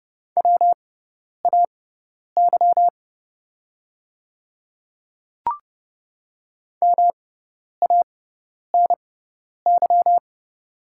Morse code, heard as a keyed tone, about 29 wpm, spaced as for 10 wpm, 690 Hz. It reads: WAY MANY
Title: 6x Character Spacing - Top 100 Words - 30wpm